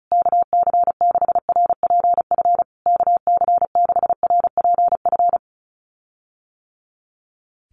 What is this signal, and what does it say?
Morse code, heard as a keyed tone, about 34 wpm, 700 Hz. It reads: KC6RPF KC6RPF
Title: USA Call Signs Repeated 3x 35wpm